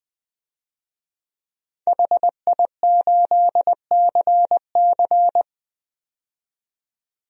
Morse code, heard as a keyed tone, about 20 wpm, 700 Hz. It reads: HI8CC
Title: DXCC-100 Call Signs - 5 Characters - 20wpm